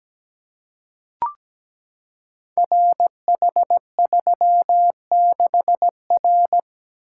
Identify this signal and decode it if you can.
Morse code, heard as a keyed tone, about 17 wpm, 700 Hz. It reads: RH36R
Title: Code Groups - 5 Letter:Numbers - 17wpm